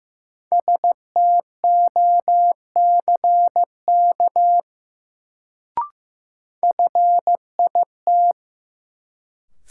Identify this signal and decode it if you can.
Morse code, heard as a keyed tone, about 15 wpm, 700 Hz. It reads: STOCK FIT
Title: Sentences from Top 800 Words - New Word Drill - 15wpm